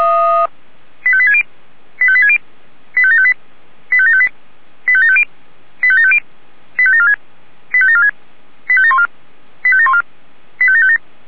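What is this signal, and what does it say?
Five-tone selective call (ZVEI), heard as quick runs of five tones, tones of about 70 ms. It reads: 85860 85860 85758 85758 85750 85750 85746 85746 85714 85714 85757
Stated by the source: ZVEI five-tone call received on an SDR.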